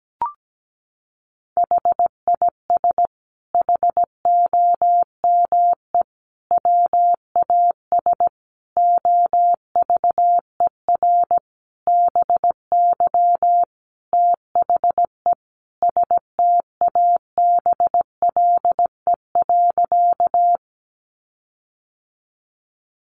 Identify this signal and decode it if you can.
Morse code, heard as a keyed tone, about 17 wpm, 700 Hz. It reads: HIS HOME WAS OVER BY THE STABLE.
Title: Sentences from Top 1500 Words 17wpm